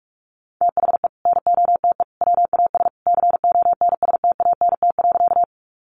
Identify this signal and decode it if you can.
Morse code, heard as a keyed tone, about 44 wpm, 700 Hz. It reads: THE NOTE WAS CONSTANT.
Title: Sentences from Top 1500 Words 45wpm